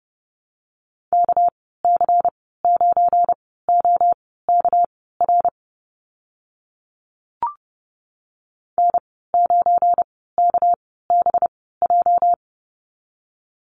Morse code, heard as a keyed tone, about 30 wpm, spaced as for 17 wpm, 700 Hz. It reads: KC9OKR N9KBJ
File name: Lesson 37g - Letter x - Call Signs 30wpm